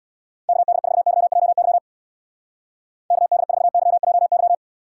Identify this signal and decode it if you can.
Morse code, heard as a keyed tone, about 66 wpm, 700 Hz. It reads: KD4YJY KD4YJY